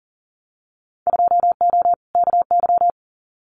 Morse code, heard as a keyed tone, about 40 wpm, 700 Hz. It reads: 2O KY
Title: Winter Field Day Exchanges 40wpm